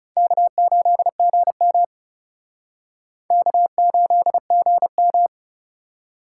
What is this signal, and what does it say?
Morse code, heard as a keyed tone, about 34 wpm, 700 Hz. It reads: K8GM K8GM
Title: Speed Racing - USA 1x2 Call Signs 30wpm